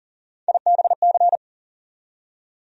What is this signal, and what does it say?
Morse code, heard as a keyed tone, about 40 wpm, 700 Hz. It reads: IBC
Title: USA Call Sign Postfixes - 3 Letters 40wpm